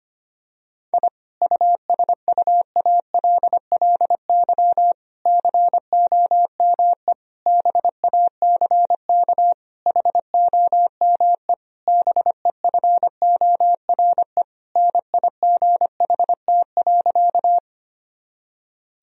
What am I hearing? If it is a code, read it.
Morse code, heard as a keyed tone, about 25 wpm, 700 Hz. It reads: I USUALLY COME BACK HOME BEFORE NIGHT.
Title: Sentences from Top 1000 Words - Review Part 3 - 25wpm